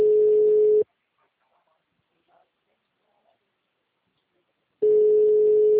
Ringback tone (Europe, 425 Hz).
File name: out- -2007- - - .3593